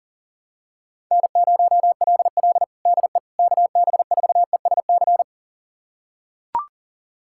Morse code, heard as a keyed tone, about 37 wpm, 700 Hz. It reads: N0LL DE KB4ESC
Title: USA Callsign DE Callsign 40wpm